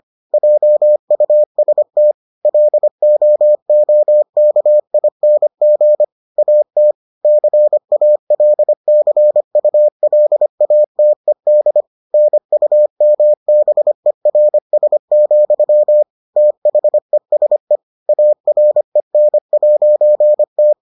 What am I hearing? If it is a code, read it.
Morse code, heard as a keyed tone, about 25 wpm, 600 Hz. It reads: JUST LOOKING AT CALCULATED NUMBERS, THESE AREN'T